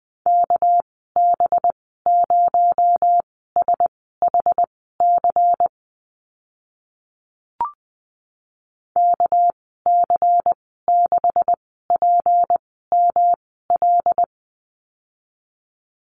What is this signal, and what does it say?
Morse code, heard as a keyed tone, about 20 wpm, spaced as for 14 wpm, 700 Hz. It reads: KB0SHC KC6PML
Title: 238 Lesson 38g - Letter q - Call Signs 20wpm